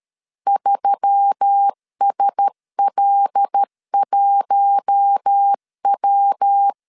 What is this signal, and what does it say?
Morse code, heard as a keyed tone, about 13 wpm, 800 Hz. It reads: 3SL1W
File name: CW / DET / 1 6 1500C S C 12wpm 3533